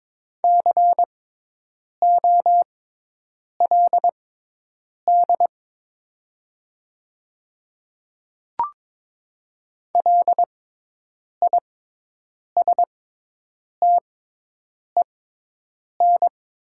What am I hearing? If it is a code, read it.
Morse code, heard as a keyed tone, about 22 wpm, spaced as for 8 wpm, 700 Hz. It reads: COLD LISTEN
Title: 6x Character Spacing - Top 500 Words - 22wpm